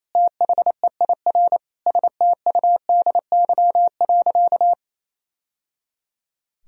Morse code, heard as a keyed tone, about 27 wpm, 700 Hz. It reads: THEIR STUDY.
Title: Sentences from Top 1000 Words - Review Part 3 - 28wpm